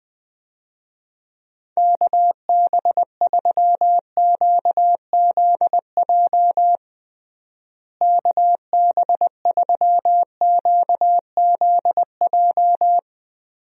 Morse code, heard as a keyed tone, about 21 wpm, 700 Hz. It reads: KB3QZJ KB3QZJ
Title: USA Call Signs - 6 Characters - Repeated 3x - 20wpm